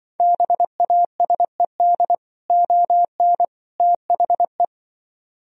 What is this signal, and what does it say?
Morse code, heard as a keyed tone, about 25 wpm, 700 Hz. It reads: BASED ON THE
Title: Speed-Racing - Sets of 3 Words 20wpm